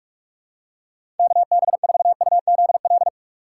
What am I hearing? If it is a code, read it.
Morse code, heard as a keyed tone, about 42 wpm, 700 Hz. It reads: KB4UZL
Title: USA Call Signs - 6x Character Spacing - 45wpm